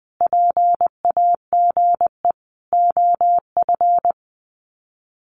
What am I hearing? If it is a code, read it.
Morse code, heard as a keyed tone, about 20 wpm, 700 Hz. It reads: PAGE OF
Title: Sets of 5 Words Encore 20wpm